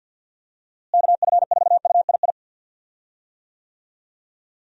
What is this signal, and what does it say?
Morse code, heard as a keyed tone, about 52 wpm, 700 Hz. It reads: KF4UII
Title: USA Call Signs - 6 Characters - Repeated 2x - 50wpm